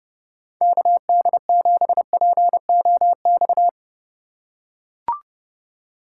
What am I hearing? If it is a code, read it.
Morse code, heard as a keyed tone, about 31 wpm, 700 Hz. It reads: KD7POX